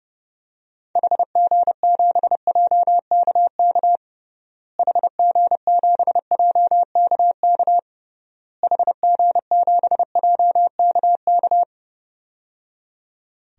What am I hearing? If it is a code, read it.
Morse code, heard as a keyed tone, about 30 wpm, 700 Hz. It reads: HG7JKK HG7JKK HG7JKK
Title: DXCC-100 Call Signs - 6 Characters - Repeated 3x - 30wpm